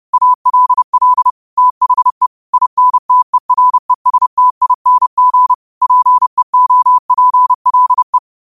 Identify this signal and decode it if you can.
Morse code, heard as a keyed tone, about 30 wpm, 1000 Hz. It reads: ALL THE INTERESTING PEOPLE
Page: New quotes every day in morse code at 30 Words per minute.